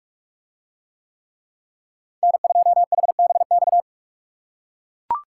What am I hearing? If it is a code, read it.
Morse code, heard as a keyed tone, about 40 wpm, 700 Hz. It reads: N2HBX